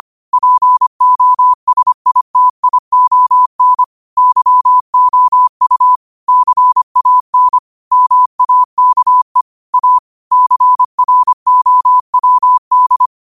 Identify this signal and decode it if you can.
Morse code, heard as a keyed tone, about 25 wpm, 1000 Hz. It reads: POSITION YOU CAN MAKE A CROWD
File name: qotdfiles / 25wpm